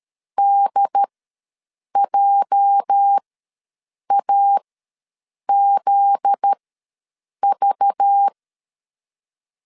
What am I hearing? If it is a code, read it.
Morse code, heard as a keyed tone, about 13 wpm, spaced as for 7 wpm, 800 Hz. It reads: DJAZV
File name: LEZIONI / 8 YJ45 / 8 YJ ws 12wpm 3342